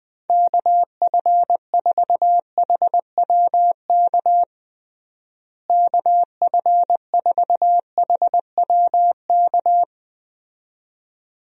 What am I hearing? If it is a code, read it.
Morse code, heard as a keyed tone, about 20 wpm, 700 Hz. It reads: KF4HWK KF4HWK
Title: USA Call Signs - 6 Characters - Repeated 2x - 20wpm